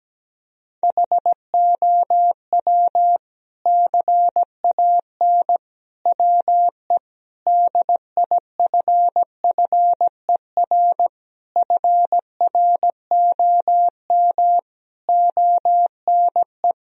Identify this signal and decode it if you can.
Morse code, heard as a keyed tone, about 17 wpm, 700 Hz. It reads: HOW CAN WE DIFFER FROM ONE
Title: Sentences from Top 1000 Words - Review Part 1 - 17wpm